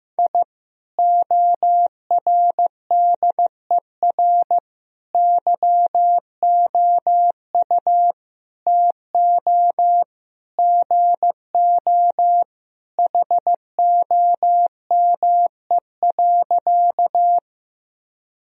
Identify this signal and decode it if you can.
Morse code, heard as a keyed tone, about 15 wpm, 700 Hz. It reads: I ORDER YOU TO GO HOME.
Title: Sentences from Top 1000 Words - Review Part 2 - 15wpm